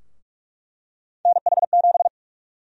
Morse code, heard as a keyed tone, about 47 wpm, 700 Hz. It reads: NH7
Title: USA Call Sign Prefixes - 45wpm